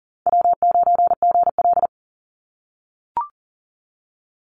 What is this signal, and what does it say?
Morse code, heard as a keyed tone, about 40 wpm, 700 Hz. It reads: W9GL
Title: Call Signs - 40wpm